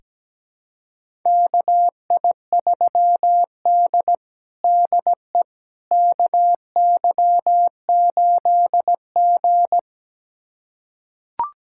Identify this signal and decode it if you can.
Morse code, heard as a keyed tone, about 17 wpm, 700 Hz. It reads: KI3D DE KY8G
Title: USA Callsign DE Callsign - 4 Characters - 17wpm